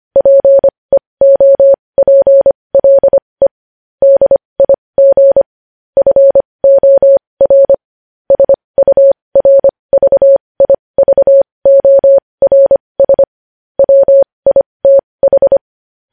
Morse code, heard as a keyed tone, about 25 wpm, spaced as for 20 wpm, 550 Hz. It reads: PEOPLE DIG FOR SURVIVORS WITH
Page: News Headlines in Morse Code at 20 WPM